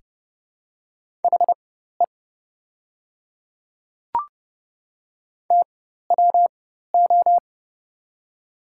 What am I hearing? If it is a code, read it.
Morse code, heard as a keyed tone, about 32 wpm, spaced as for 14 wpm, 700 Hz. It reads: HE TWO
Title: 4x Character Spacing - Top 300 Words - 30wpm